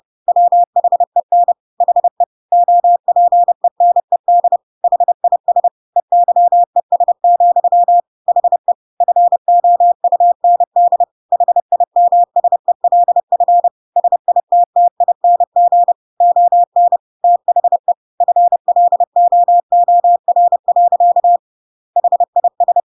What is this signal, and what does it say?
Morse code, heard as a keyed tone, about 30 wpm, 700 Hz. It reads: WHEN HE OPENED HIS EYES, HE FOUND HIMSELF SITTING ON THE FLOOR. HIS
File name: Pinocchio / Pinocchio ch1 30 30 ik1hge